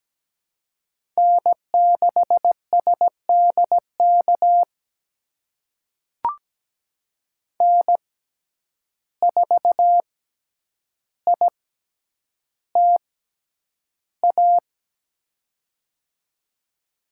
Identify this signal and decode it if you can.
Morse code, heard as a keyed tone, about 17 wpm, spaced as for 9 wpm, 700 Hz. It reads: N6SDK N4ITA
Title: USA Call Signs - 5 Characters - 6x Character Spacing - 17wpm